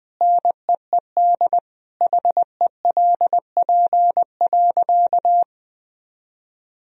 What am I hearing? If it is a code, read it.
Morse code, heard as a keyed tone, about 20 wpm, 700 Hz. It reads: NEED HELP.
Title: Sentences from Top 1000 Words - Review Part 1 - 20wpm